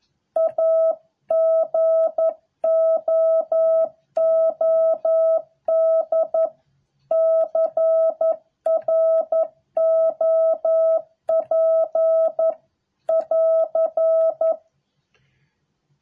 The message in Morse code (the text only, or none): A GOOD CROP <AR>